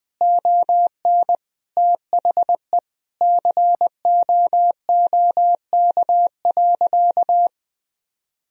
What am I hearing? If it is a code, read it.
Morse code, heard as a keyed tone, about 21 wpm, 700 Hz. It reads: ON THE COOK.
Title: Sentences from Top 1000 Words - Review Part 4 - 20wpm